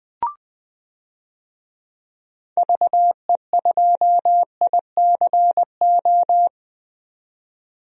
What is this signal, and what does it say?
Morse code, heard as a keyed tone, about 20 wpm, 700 Hz. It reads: VE2ICO
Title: DXCC-100 Call Signs - 6 Characters - 20wpm